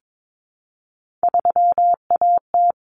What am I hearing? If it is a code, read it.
Morse code, heard as a keyed tone, about 22 wpm, 700 Hz. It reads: 3AT
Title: Common Cut Numbers - 3-Digit 22wpm